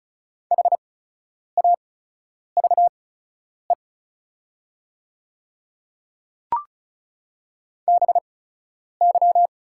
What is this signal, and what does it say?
Morse code, heard as a keyed tone, about 34 wpm, spaced as for 10 wpm, 690 Hz. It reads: HAVE BY
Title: 8x Character Spacing - Top 100 Words - 35wpm